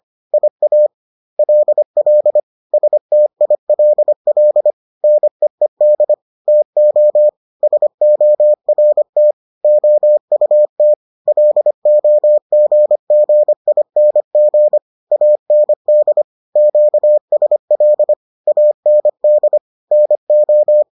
Morse code, heard as a keyed tone, about 25 wpm, 600 Hz. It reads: IALL STILL NEED TO SORT OUT LOGGING AND QSL AND NO